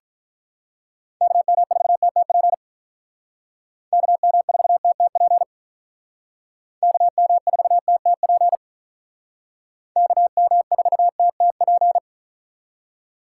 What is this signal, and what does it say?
Morse code, heard as a keyed tone, about 48 wpm, 700 Hz. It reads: KM4TTP KM4TTP KM4TTP KM4TTP